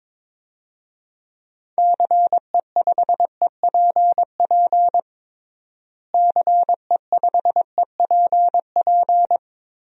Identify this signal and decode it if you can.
Morse code, heard as a keyed tone, about 22 wpm, 700 Hz. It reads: CE5EPP CE5EPP
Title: DXCC-100 Call Signs - 6 Characters - Repeated 3x - 22wpm